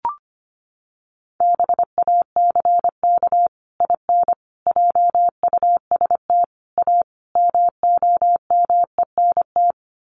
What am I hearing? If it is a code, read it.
Morse code, heard as a keyed tone, about 25 wpm, 700 Hz. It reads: BACK IN JUST A MOMENT
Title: Sets of 5 Words Encore 25wpm